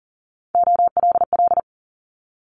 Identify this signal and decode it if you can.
Morse code, heard as a keyed tone, about 40 wpm, 700 Hz. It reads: OLL
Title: Multi-letter Phonemes Quick Fire 40wpm